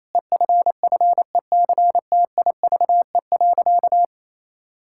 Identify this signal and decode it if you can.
Morse code, heard as a keyed tone, about 28 wpm, 700 Hz. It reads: EFFECTIVE.
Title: Sentences from Top 1600 Words 28wpm